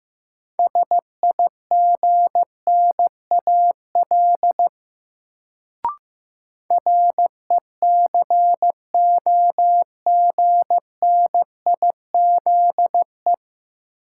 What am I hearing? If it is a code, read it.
Morse code, heard as a keyed tone, about 15 wpm, 700 Hz. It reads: SIGNAL RECOGNIZE
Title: Sentences from Top 1400 Words - New Word Drill - 15wpm